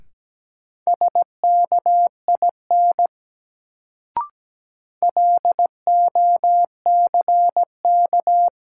Morse code, heard as a keyed tone, about 17 wpm, 700 Hz. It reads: SKIN LOCK
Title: Sentences from Top 1300 Words - New Word Drill Alt - 17wpm